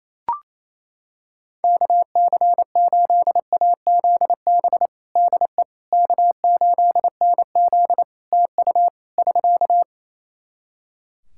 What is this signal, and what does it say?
Morse code, heard as a keyed tone, about 28 wpm, 700 Hz. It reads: KC8AZB DE K8NZ TU <SK>
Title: Practice QSO - Exchange 4 - Ending - 28wpm